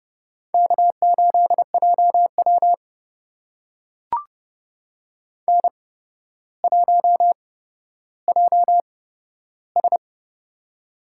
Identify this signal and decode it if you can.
Morse code, heard as a keyed tone, about 29 wpm, spaced as for 13 wpm, 700 Hz. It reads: K8JW N1JS
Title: USA Call Signs - 4 Characters - 8x Character Spacing - 30wpm